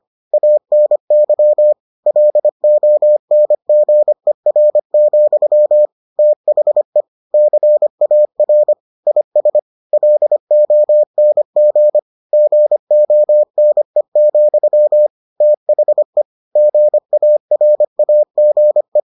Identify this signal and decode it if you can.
Morse code, heard as a keyed tone, about 25 wpm, 600 Hz. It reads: ANY LONGER, THE CAR IS LONG GONE, THE GARAGE